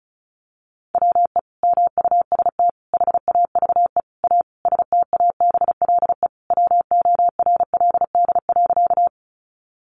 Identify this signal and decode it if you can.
Morse code, heard as a keyed tone, about 34 wpm, 700 Hz. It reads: WE MUST HAVE A STABLE WORLD.
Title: Sentences from Top 1500 Words 35wpm